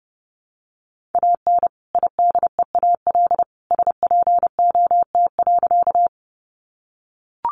Morse code, heard as a keyed tone, about 30 wpm, 700 Hz. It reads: AN IDEAL SPOT.
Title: Sentences from Top 1400 Words 30wpm